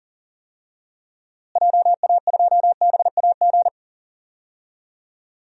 Morse code, heard as a keyed tone, about 40 wpm, 700 Hz. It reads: JA2BAG
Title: DXCC-100 Call Signs - 6 Characters - Repeated 3x - 40wpm